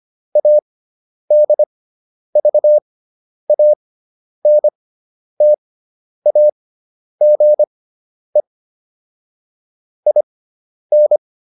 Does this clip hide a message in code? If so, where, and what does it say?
Yes: Morse code from 0.4 to 11.2 s, heard as a keyed tone, about 25 wpm, spaced as for 10 wpm, 600 Hz. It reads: ADVANTAGE IN